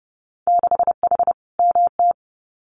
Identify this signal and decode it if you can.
Morse code, heard as a keyed tone, about 30 wpm, 700 Hz. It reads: 6H MT
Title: Winter Field Day Exchanges Encore 30wpm